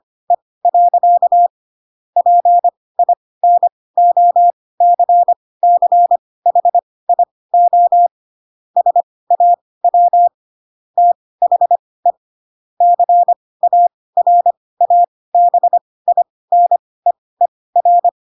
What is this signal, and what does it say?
Morse code, heard as a keyed tone, about 25 wpm, spaced as for 18 wpm, 700 Hz. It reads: E. PINOCCHIO SAW THE CARABINEER
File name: Pinocchio ch3 12 25 ik1hge